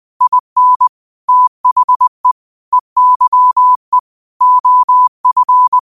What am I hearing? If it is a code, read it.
Morse code, heard as a keyed tone, about 20 wpm, 1000 Hz. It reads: IN THE EYE OF
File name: qotdfiles / 20wpm